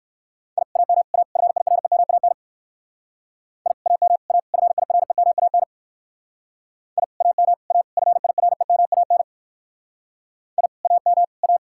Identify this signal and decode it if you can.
Morse code, heard as a keyed tone, about 59 wpm, 700 Hz. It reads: I AM A FIREMAN I AM A FIREMAN I AM A FIREMAN I AM A